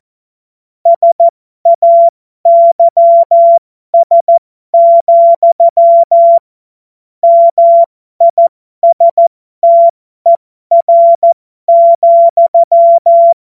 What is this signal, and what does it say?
Morse code, heard as a keyed tone, about 14 wpm, spaced as for 12 wpm, 680 Hz. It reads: SAYS, MISTER,
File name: Morsekurs / Witze